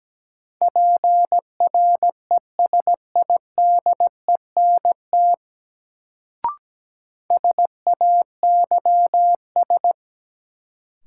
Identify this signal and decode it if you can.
Morse code, heard as a keyed tone, about 17 wpm, 700 Hz. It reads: PRESIDENT SAYS
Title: Sets of 2 Words - New Word Drill - 17wpm